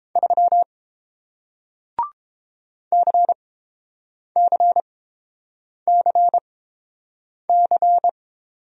Morse code, heard as a keyed tone, about 31 wpm, spaced as for 7 wpm, 700 Hz. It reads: 3 CCCC